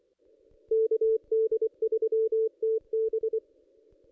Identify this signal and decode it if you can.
Morse code, heard as a keyed tone, about 24 wpm, 440 Hz. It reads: KD3TB
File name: contests / ss-cw-k5tr-op-n5rz -1600